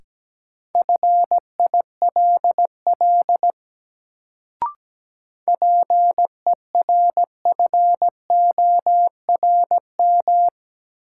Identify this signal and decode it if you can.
Morse code, heard as a keyed tone, about 17 wpm, 700 Hz. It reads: FILL PERFORM
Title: Sentences from Top 1500 Words - New Word Drill Alt - 17wpm